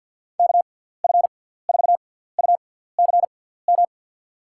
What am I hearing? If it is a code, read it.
Morse code, heard as a keyed tone, about 50 wpm, 700 Hz. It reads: K F 4 U C M
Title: USA Call Signs - 6 Characters - 6x Character Spacing - 50wpm